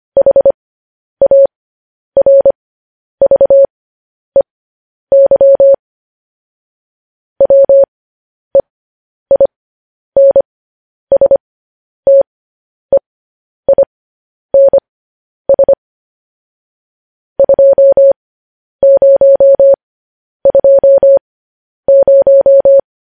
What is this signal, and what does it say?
Morse code, heard as a keyed tone, about 25 wpm, spaced as for 10 wpm, 550 Hz. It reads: HARVEY WEINSTEINS 2020
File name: podcast / News headlines-1x-25x10